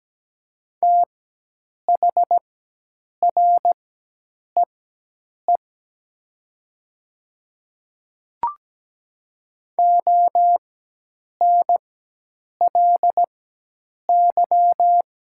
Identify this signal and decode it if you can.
Morse code, heard as a keyed tone, about 17 wpm, spaced as for 8 wpm, 700 Hz. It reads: THREE ONLY
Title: 4x Character Spacing - Top 500 Words - 17wpm